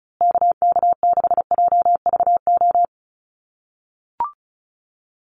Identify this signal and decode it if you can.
Morse code, heard as a keyed tone, about 34 wpm, 700 Hz. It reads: KK6JVO
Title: USA Call Signs Repeated 3x 35wpm